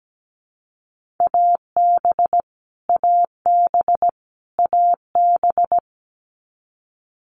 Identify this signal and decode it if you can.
Morse code, heard as a keyed tone, about 17 wpm, 700 Hz. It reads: AB AB AB
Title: Canadian Provinces and Territories - Repeated 3x - 17wpm